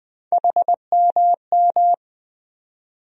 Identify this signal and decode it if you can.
Morse code, heard as a keyed tone, about 20 wpm, 700 Hz. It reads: HMM
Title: USA Call Sign Postfixes - 3 Letters 20wpm